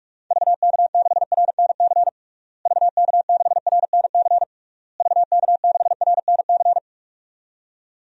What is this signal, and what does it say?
Morse code, heard as a keyed tone, about 46 wpm, 700 Hz. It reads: VK6RNC VK6RNC VK6RNC
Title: DXCC-100 Call Signs - 6 Characters - Repeated 3x - 45wpm